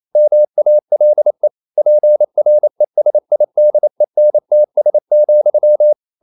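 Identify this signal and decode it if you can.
Morse code, heard as a keyed tone, about 28 wpm, 600 Hz. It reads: MALE PRESIDENTS,
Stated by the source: Morse code practice at 28 wpm, text of Ken Follett's novel Never in English, from CW med Gnister.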